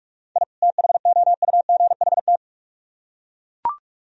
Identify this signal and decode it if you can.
Morse code, heard as a keyed tone, about 42 wpm, 700 Hz. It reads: I THOUGHT